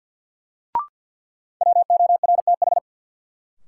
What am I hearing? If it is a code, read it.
Morse code, heard as a keyed tone, about 52 wpm, 700 Hz. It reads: WORTH
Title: Sentences from Top 800 Words - New Word Drill - 50wpm